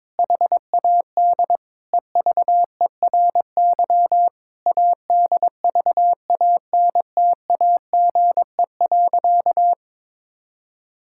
Morse code, heard as a keyed tone, about 22 wpm, 700 Hz. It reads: HAD EVERY ADVANTAGE.
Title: Sentences from Top 1000 Words - Review Part 6 - 22wpm